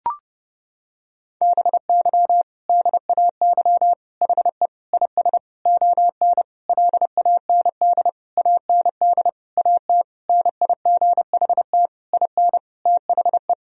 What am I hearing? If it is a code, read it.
Morse code, heard as a keyed tone, about 31 wpm, 700 Hz. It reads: BY DAY HE IS ON LAND AND AT NIGHT IN THE
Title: Sentences from Top 1000 Words - Review Part 1 - 30wpm